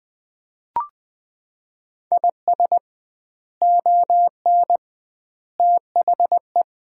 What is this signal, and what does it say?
Morse code, heard as a keyed tone, about 20 wpm, 700 Hz. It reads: IS ON THE
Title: 2x Word-Spacing - Sets of 3 Words - 20wpm